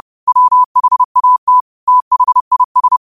Morse code, heard as a keyed tone, about 30 wpm, 1000 Hz. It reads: WHAT THIS
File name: qotdfiles / 30wpm